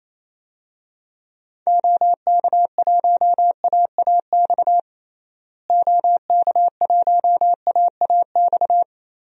Morse code, heard as a keyed tone, about 28 wpm, 700 Hz. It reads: OK1AAX OK1AAX
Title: DXCC-100 Call Signs - 6 Characters - Repeated 2x - 28wpm